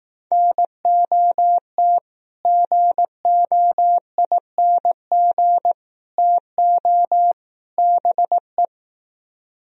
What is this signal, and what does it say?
Morse code, heard as a keyed tone, about 19 wpm, 700 Hz. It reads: NOT GOING TO BE